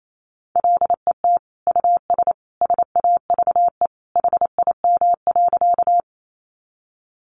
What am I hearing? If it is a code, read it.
Morse code, heard as a keyed tone, about 28 wpm, 700 Hz. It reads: LET US SAVE HIM.
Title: Sentences from Top 1100 Words 28wpm